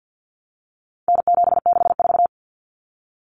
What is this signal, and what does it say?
Morse code, heard as a keyed tone, about 50 wpm, 700 Hz. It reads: N764
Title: Common Cut Numbers - 4-Digit 50wpm